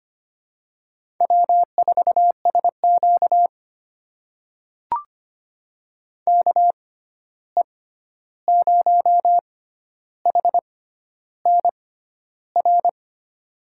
Morse code, heard as a keyed tone, about 25 wpm, spaced as for 11 wpm, 700 Hz. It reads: W4SQ KE0HNR
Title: USA Call Signs - 6x Character Spacing - 25wpm